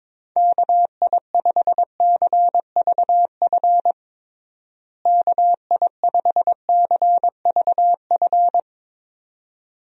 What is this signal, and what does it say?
Morse code, heard as a keyed tone, about 22 wpm, 700 Hz. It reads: KI5CVF KI5CVF
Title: USA Call Signs - 6 Characters - Repeated 2x - 22wpm